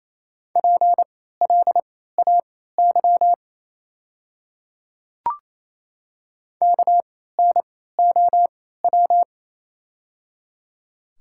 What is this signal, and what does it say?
Morse code, heard as a keyed tone, about 28 wpm, spaced as for 16 wpm, 700 Hz. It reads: PLAY KNOW
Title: 3x Character Spacing - Top 200 Words - 28wpm